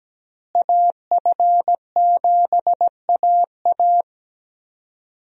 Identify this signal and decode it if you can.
Morse code, heard as a keyed tone, about 17 wpm, 700 Hz. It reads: AF7AA
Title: USA Call Signs - 5 Characters - Repeated 3x - 17wpm